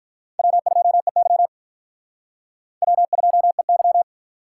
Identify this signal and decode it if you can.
Morse code, heard as a keyed tone, about 48 wpm, 700 Hz. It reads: W2EY W2EY